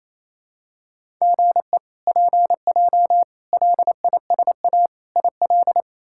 Morse code, heard as a keyed tone, about 28 wpm, 700 Hz. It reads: GE PJ LISA IL
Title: K1USN SST QSO Practice - 28wpm